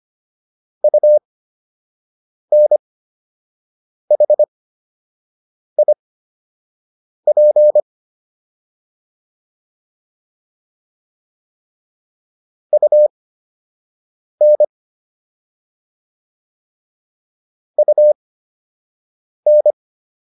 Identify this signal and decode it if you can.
Morse code, heard as a keyed tone, about 25 wpm, spaced as for 6 wpm, 600 Hz. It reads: UNHIP UN UN